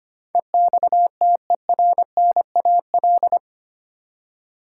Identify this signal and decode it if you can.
Morse code, heard as a keyed tone, about 25 wpm, 700 Hz. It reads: EXTERNAL
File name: Eight Letter Words - 25wpm